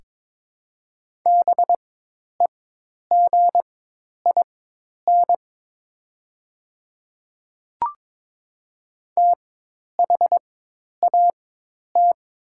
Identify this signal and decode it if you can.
Morse code, heard as a keyed tone, about 22 wpm, spaced as for 10 wpm, 690 Hz. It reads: BEGIN THAT
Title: 4x Character Spacing - Top 300 Words - 22wpm